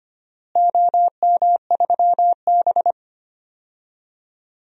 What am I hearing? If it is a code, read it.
Morse code, heard as a keyed tone, about 25 wpm, 700 Hz. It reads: OM3B
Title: DXCC-100 Call Signs - 4 Characters - Repeated 2x - 25wpm